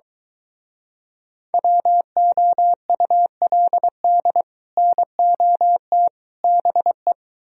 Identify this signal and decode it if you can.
Morse code, heard as a keyed tone, about 23 wpm, 700 Hz. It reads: WOULD NOT BE